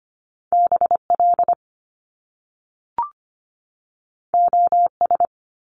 Morse code, heard as a keyed tone, about 25 wpm, 700 Hz. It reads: BL OS
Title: Top Two Letter Combinations in Words 25wpm